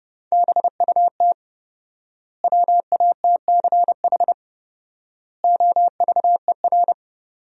Morse code, heard as a keyed tone, about 30 wpm, 700 Hz. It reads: BUT WATCH OVER
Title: 4x Word-Spacing - Sentences from Top 300 Words - 30wpm